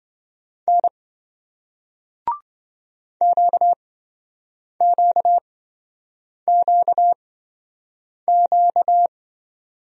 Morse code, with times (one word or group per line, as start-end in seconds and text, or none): 0.7-0.9 N
3.2-9.1 QQQQ